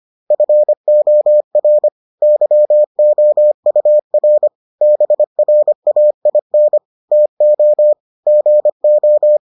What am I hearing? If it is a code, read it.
Morse code, heard as a keyed tone, about 25 wpm, 600 Hz. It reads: FOR YOUR BRAIN TO GO